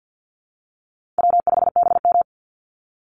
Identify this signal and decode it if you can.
Morse code, heard as a keyed tone, about 49 wpm, 700 Hz. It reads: W5BM
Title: Easy Call Signs Repeated 2x - 50wpm